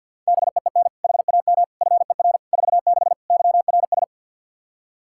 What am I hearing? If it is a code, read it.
Morse code, heard as a keyed tone, about 51 wpm, 700 Hz. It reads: BEEN HAM FER 46 YRS
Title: K1USN SST QSO Practice 50wpm